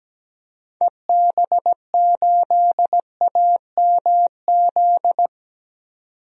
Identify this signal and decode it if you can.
Morse code, heard as a keyed tone, about 17 wpm, 700 Hz. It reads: EB8AMZ
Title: DXCC-100 Call Signs - 6 Characters - 17wpm